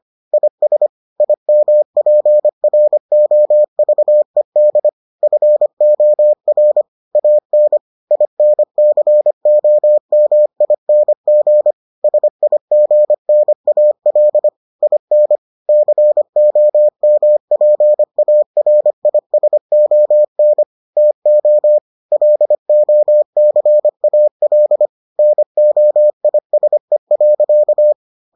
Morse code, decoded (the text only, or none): IS IMPROVED FOR AN INCOMING SIGNAL IN COMPARISON TO LOCAL NOISE.